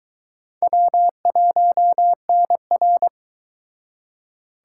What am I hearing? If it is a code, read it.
Morse code, heard as a keyed tone, about 23 wpm, 700 Hz. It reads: W1NR